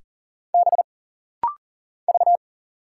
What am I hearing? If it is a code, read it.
Morse code, heard as a keyed tone, about 40 wpm, 700 Hz. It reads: B V
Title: Single Letter-Number - Rapid-Fire 40wpm